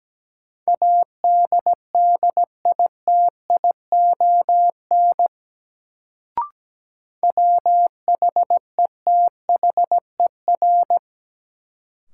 Morse code, heard as a keyed tone, about 17 wpm, 700 Hz. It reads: ADDITION WHETHER
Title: Sets of 3 Words - New Word Drill - 17wpm